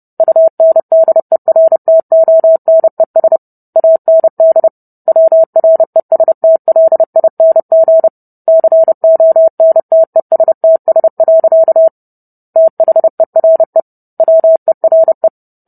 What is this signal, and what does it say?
Morse code, heard as a keyed tone, about 30 wpm, 650 Hz. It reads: UNDERTONES AND WRESTLING CONTESTS. THERE WERE